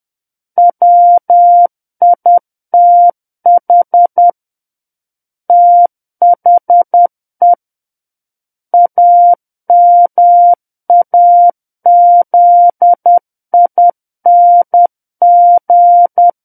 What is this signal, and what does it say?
Morse code, heard as a keyed tone, about 10 wpm, 700 Hz. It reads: WITH THE AMAZING
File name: Book One / 01-Chapter 1 - 10 WPM